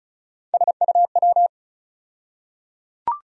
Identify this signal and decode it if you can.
Morse code, heard as a keyed tone, about 36 wpm, 700 Hz. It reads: SUW
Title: USA Call Sign Postfixes - 3 Letters 35wpm